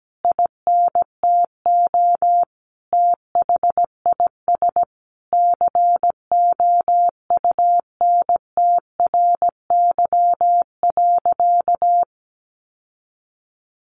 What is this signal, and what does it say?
Morse code, heard as a keyed tone, about 17 wpm, 700 Hz. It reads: INTO THIS COUNTRY.
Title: Sentences from Top 1400 Words 17wpm